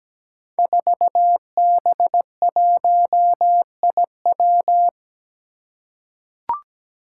Morse code, heard as a keyed tone, about 17 wpm, 700 Hz. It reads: 4B1IW
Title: Five Challenging Letter-Number Combinations 17wpm